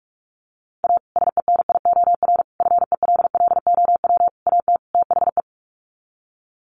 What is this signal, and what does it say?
Morse code, heard as a keyed tone, about 44 wpm, 700 Hz. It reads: A SENIOR FELLOW AT THE
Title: Sets of 5 Words Encore 45wpm